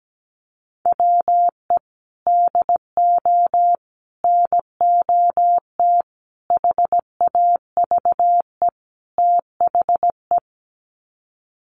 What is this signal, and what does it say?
Morse code, heard as a keyed tone, about 17 wpm, 700 Hz. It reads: WE DO NOT HAVE THE
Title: Sets of 5 Words 17wpm